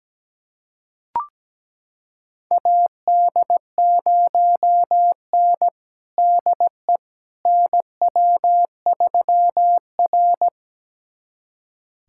Morse code, heard as a keyed tone, about 17 wpm, 700 Hz. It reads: AD0N DE NW3R
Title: USA Callsign DE Callsign - 4 Characters - 17wpm